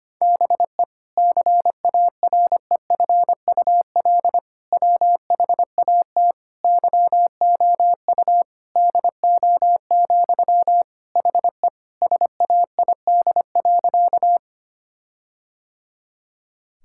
Morse code, heard as a keyed tone, about 25 wpm, 700 Hz. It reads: BE CAREFUL WHAT YOU DO, HE SAID.
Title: Sentences from Top 1600 Words 25wpm